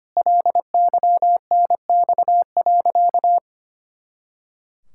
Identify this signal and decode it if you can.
Morse code, heard as a keyed tone, about 25 wpm, 700 Hz. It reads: LYNX.
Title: Pangram Challenge v2 - 25wpm